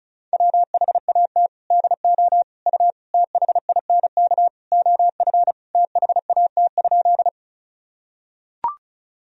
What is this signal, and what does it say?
Morse code, heard as a keyed tone, about 36 wpm, 700 Hz. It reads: WHAT DO U THINK OF THAT?
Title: Sentences from Top 200 Words and Abbreviations 35wpm